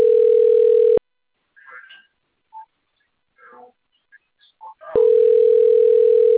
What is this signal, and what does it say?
Ringback tone (North America), 440 + 480 Hz.